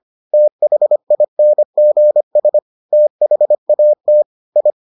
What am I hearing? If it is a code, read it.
Morse code, heard as a keyed tone, about 25 wpm, 600 Hz. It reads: THINGS THAT I